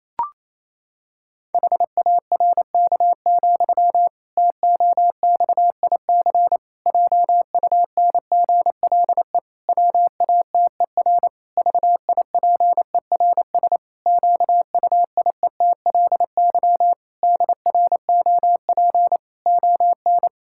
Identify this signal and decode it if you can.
Morse code, heard as a keyed tone, about 27 wpm, 700 Hz. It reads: HARK, TOXIC JUNGLE WATER VIPERS QUIETLY DROP ON
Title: Pangram Challenge v2 - 28wpm